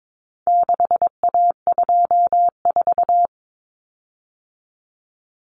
Morse code, heard as a keyed tone, about 22 wpm, 700 Hz. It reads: 6A24
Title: Common Cut Numbers - 4-Digit 22wpm